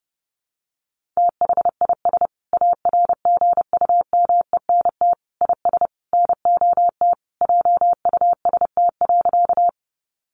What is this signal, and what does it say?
Morse code, heard as a keyed tone, about 28 wpm, 700 Hz. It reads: THIS ARGUMENT IS NOT JUST.
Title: Sentences from Top 1400 Words 30wpm